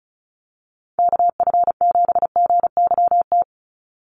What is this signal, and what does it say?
Morse code, heard as a keyed tone, about 34 wpm, 700 Hz. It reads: KF7GYT
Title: USA Call Signs 35wpm